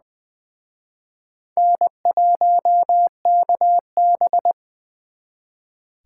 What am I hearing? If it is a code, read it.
Morse code, heard as a keyed tone, about 23 wpm, 700 Hz. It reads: N1KB
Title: Speed Racing - USA 1x2 Call Signs 20wpm